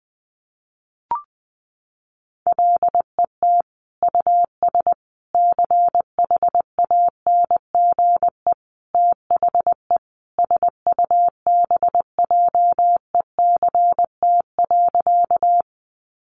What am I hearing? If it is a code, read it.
Morse code, heard as a keyed tone, about 20 wpm, 700 Hz. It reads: LET US CHANGE THE SUBJECT.
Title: Sentences from Top 800 Words 20wpm